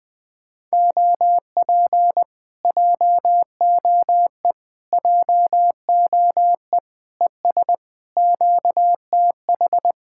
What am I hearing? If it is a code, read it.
Morse code, heard as a keyed tone, about 20 wpm, 700 Hz. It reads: OP JOE JOE ES QTH